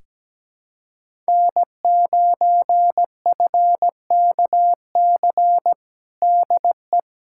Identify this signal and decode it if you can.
Morse code, heard as a keyed tone, about 17 wpm, 700 Hz. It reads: N9FKC DE
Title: USA Callsign DE Callsign 17wpm